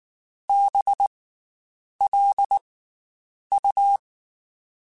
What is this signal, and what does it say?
Morse code, heard as a keyed tone, about 19 wpm, 800 Hz. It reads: B L U
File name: Special INGs / Special INGs-mp3s / Slow ing @18